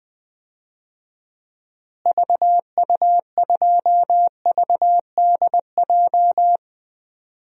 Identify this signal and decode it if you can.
Morse code, heard as a keyed tone, about 21 wpm, 700 Hz. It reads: VU2VDJ
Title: DXCC-100 Call Signs - 6 Characters - Repeated 2x - 20wpm